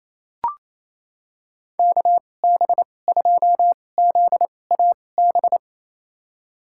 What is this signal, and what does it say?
Morse code, heard as a keyed tone, about 28 wpm, spaced as for 20 wpm, 700 Hz. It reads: KB2ZAB
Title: USA Call Signs - 2x Character Spacing - 28wpm